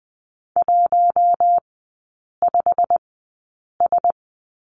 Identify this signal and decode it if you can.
Morse code, heard as a keyed tone, about 20 wpm, spaced as for 8 wpm, 700 Hz. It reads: 15S
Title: 107 Lesson 18e - Letter m - Three Characters 20wpm